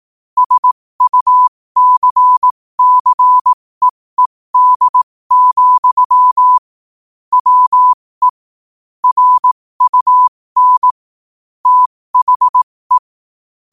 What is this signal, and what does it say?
Morse code, heard as a keyed tone, about 18 wpm, spaced as for 15 wpm, 1000 Hz. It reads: SUCCEED, WE RUN THE
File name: qotdfiles / 15wpm